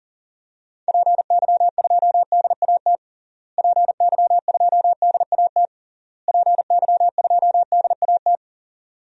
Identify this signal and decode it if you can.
Morse code, heard as a keyed tone, about 40 wpm, 700 Hz. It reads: PY2DAT PY2DAT PY2DAT
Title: DXCC-100 Call Signs - 6 Characters - Repeated 3x - 40wpm